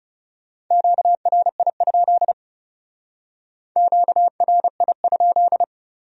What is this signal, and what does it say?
Morse code, heard as a keyed tone, about 34 wpm, 700 Hz. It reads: QRI? QRI?
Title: Speed-Racing Q-Codes - 30wpm